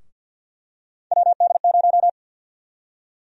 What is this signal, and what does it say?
Morse code, heard as a keyed tone, about 53 wpm, 700 Hz. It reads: WD0
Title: USA Call Sign Prefixes - 50wpm